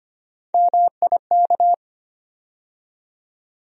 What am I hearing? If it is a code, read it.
Morse code, heard as a keyed tone, about 25 wpm, 700 Hz. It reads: MIK
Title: USA Call Sign Postfixes - 3 Letters 25wpm